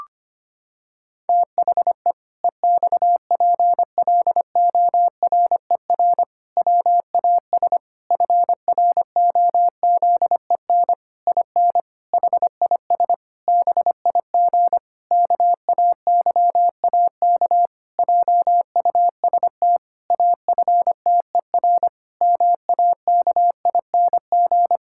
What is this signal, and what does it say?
Morse code, heard as a keyed tone, about 25 wpm, 700 Hz. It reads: THE EXPLORER WAS FROZEN IN HIS BIG KAYAK JUST AFTER MAKING
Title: Pangram Challenge v2 - 25wpm